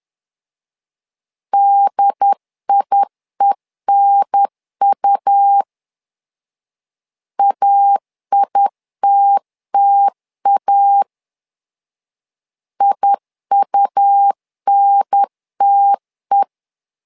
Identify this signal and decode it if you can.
Morse code, heard as a keyed tone, about 10 wpm, 800 Hz. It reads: DIENU AITTA IUNTE
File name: LEZIONI / 2 UHDN / 2 UHDN 10wpm 1904